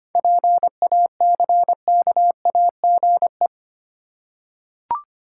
Morse code, heard as a keyed tone, about 25 wpm, 700 Hz. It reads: PACKAGE
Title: Seven Letter Words - 25wpm